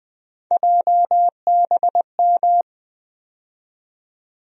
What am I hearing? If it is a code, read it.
Morse code, heard as a keyed tone, about 20 wpm, 700 Hz. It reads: JBM
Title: USA Call Sign Postfixes - 3 Letters 20wpm